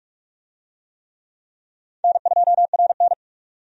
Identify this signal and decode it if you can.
Morse code, heard as a keyed tone, about 43 wpm, 700 Hz. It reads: N2RN
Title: USA Call Signs - 4 Characters - 45wpm